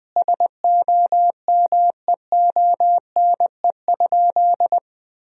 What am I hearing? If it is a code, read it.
Morse code, heard as a keyed tone, about 20 wpm, 700 Hz. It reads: SOMEONE?
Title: Sentences from Top 1000 Words - Review Part 4 - 20wpm